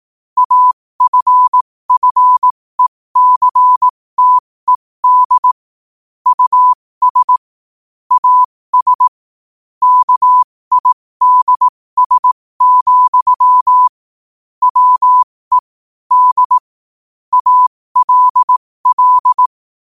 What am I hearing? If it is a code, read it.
Morse code, heard as a keyed tone, about 18 wpm, spaced as for 15 wpm, 1000 Hz. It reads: AFFECTED US AS KIDS, WE D ALL
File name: qotdfiles / 15wpm